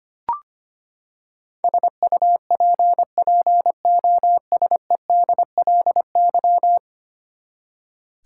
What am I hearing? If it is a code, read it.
Morse code, heard as a keyed tone, about 25 wpm, 700 Hz. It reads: SUPPOSEDLY
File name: Ten Letter Words - 25wpm